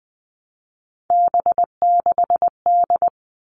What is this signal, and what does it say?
Morse code, heard as a keyed tone, about 20 wpm, 700 Hz. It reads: B6D
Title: All Cut Numbers - 3-Digit 20wpm